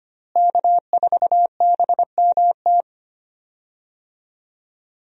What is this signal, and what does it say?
Morse code, heard as a keyed tone, about 25 wpm, 700 Hz. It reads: K4BMT
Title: USA Call Signs - 5 Characters - Repeated 3x - 25wpm